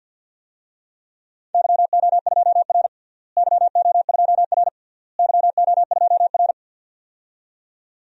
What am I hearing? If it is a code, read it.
Morse code, heard as a keyed tone, about 49 wpm, 700 Hz. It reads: YO2R YO2R YO2R
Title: DXCC-100 Call Signs - 4 Characters - Repeated 3x - 50wpm